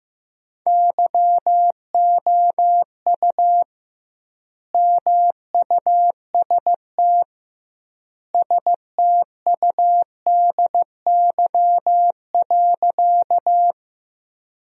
Morse code, heard as a keyed tone, about 15 wpm, 700 Hz. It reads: YOU MUST STUDY.
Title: 2x Word-Spacing - Sentences from Top 300 Words - 15wpm